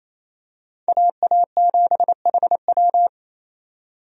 Morse code, heard as a keyed tone, about 28 wpm, 700 Hz. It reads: AA7HW
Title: USA Call Signs - 5 Characters - 28wpm